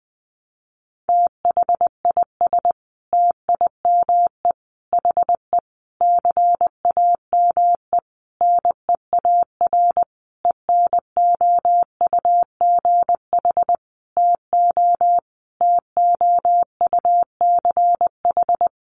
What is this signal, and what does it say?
Morse code, heard as a keyed tone, about 20 wpm, 700 Hz. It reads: THIS TIME HE CAME NEAR ENOUGH TO TOUCH
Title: Sentences from Top 1200 Words 20wpm